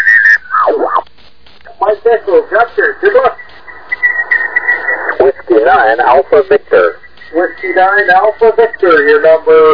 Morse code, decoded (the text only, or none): none